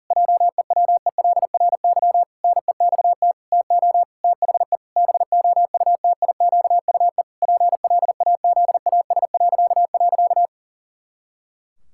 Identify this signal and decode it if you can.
Morse code, heard as a keyed tone, about 40 wpm, 700 Hz. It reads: JEWELRY NEXT TO THE BOUTIQUE PLAZAS..
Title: Pangram Challenge v2 - 40wpm